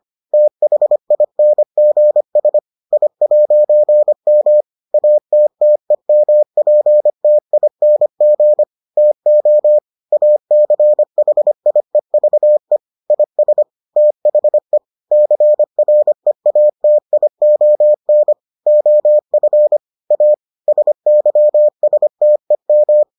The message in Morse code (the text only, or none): THINGS I'M ATTEMPTING TO ACHIEVE IS THE CREATION OF A SYSTEM